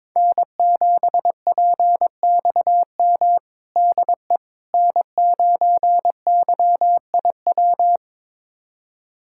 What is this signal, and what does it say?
Morse code, heard as a keyed tone, about 22 wpm, 700 Hz. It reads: N7PXM DE N9YIW
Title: USA Callsign DE Callsign - 5 Characters - 22wpm